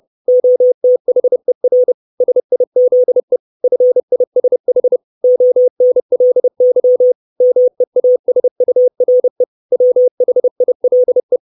Morse code, heard as a keyed tone, about 30 wpm, 500 Hz. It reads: OTHER SIZE FISH ONLY MEASURE WHILE
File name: 632 Words 30WPM